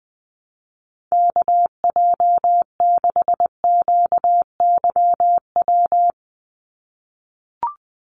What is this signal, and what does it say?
Morse code, heard as a keyed tone, about 20 wpm, 700 Hz. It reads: KJ6QYW
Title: USA Call Signs 20wpm